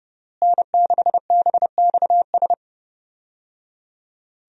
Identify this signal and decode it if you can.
Morse code, heard as a keyed tone, about 30 wpm, 700 Hz. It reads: N6BXS
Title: USA Call Signs - 5 Characters - Repeated 3x - 30wpm